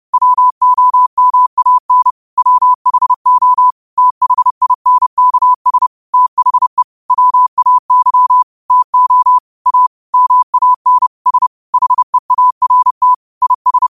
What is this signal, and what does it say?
Morse code, heard as a keyed tone, about 30 wpm, 1000 Hz. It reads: WOMAN WHO THINKS THE WAY TO A MAN S HEART IS
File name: qotdfiles / 30wpm